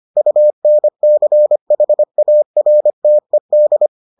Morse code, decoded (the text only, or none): UNCHARTED